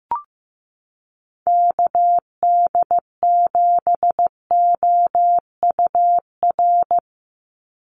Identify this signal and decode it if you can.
Morse code, heard as a keyed tone, about 15 wpm, 700 Hz. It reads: KD7OUR
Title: USA Call Signs 15wpm